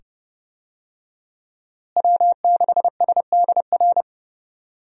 Morse code, heard as a keyed tone, about 32 wpm, 700 Hz. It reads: W6SDR
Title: USA Call Signs - 5 Characters - Repeated 2x - 30wpm